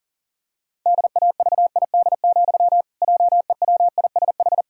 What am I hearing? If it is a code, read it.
Morse code, heard as a keyed tone, about 37 wpm, 700 Hz. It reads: DAVID, JEWISH
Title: Pangram Challenge v2 - 40wpm